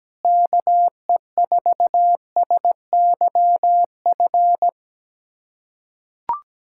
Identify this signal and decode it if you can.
Morse code, heard as a keyed tone, about 17 wpm, 700 Hz. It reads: KE4SYF
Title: USA Callsign DE Callsign - 6 Characters - 17wpm